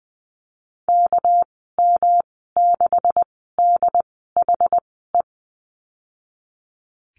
Morse code, heard as a keyed tone, about 20 wpm, spaced as for 14 wpm, 700 Hz. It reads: KM6DHE
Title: 213 Lesson 34g - Number 8 - Call Signs 20wpm